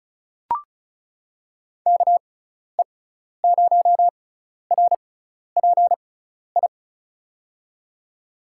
Morse code, heard as a keyed tone, about 36 wpm, spaced as for 12 wpm, 690 Hz. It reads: KE0RPI
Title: USA Call Signs - 6 Characters - 6x Character Spacing - 35wpm